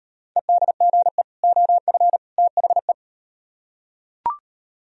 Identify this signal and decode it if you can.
Morse code, heard as a keyed tone, about 39 wpm, 700 Hz. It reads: EDGE OF THE